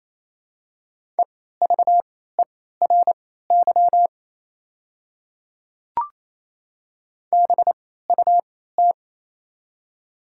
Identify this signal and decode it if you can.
Morse code, heard as a keyed tone, about 29 wpm, spaced as for 16 wpm, 700 Hz. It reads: EVERY BUT
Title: 3x Character Spacing - Top 300 Words - 28wpm